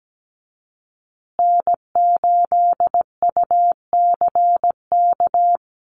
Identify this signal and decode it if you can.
Morse code, heard as a keyed tone, about 17 wpm, 700 Hz. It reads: N8UCK
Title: USA Call Signs Repeated 2x 17wpm